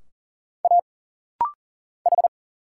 Morse code, heard as a keyed tone, about 40 wpm, 700 Hz. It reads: A H
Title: Single Letter-Number - Rapid-Fire 40wpm